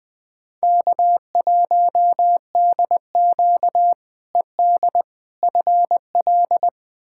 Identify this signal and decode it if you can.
Morse code, heard as a keyed tone, about 20 wpm, 700 Hz. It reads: K1DQ ED FL
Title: K1USN SST QSO Practice - 20wpm